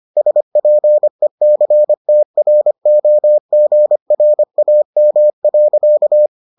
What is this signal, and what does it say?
Morse code, heard as a keyed tone, about 25 wpm, 600 Hz. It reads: SPECTROGRAM.